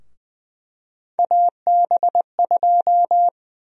Morse code, heard as a keyed tone, about 20 wpm, 700 Hz. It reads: AB2
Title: USA Call Sign Prefixes - 20wpm